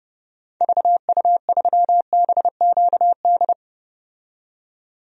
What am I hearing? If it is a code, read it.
Morse code, heard as a keyed tone, about 32 wpm, 700 Hz. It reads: VU3BQD
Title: DXCC-100 Call Signs - 6 Characters - 30wpm